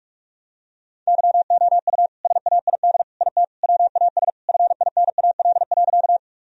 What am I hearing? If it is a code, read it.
Morse code, heard as a keyed tone, about 46 wpm, 700 Hz. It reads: YOU SAID IT WAS FINAL.
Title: Sentences from Top 1000 Words - Review Part 2 - 45wpm